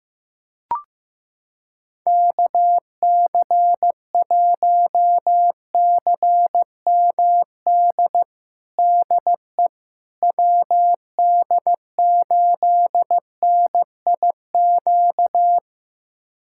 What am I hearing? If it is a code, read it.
Morse code, heard as a keyed tone, about 16 wpm, 700 Hz. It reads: KC1CMD DE WD8NIQ
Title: USA Callsign DE Callsign 15wpm